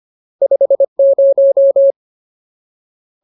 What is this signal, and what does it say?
Morse code, heard as a keyed tone, about 25 wpm, 550 Hz. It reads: 50
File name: Ultimate Word List / 25WPM Ultimate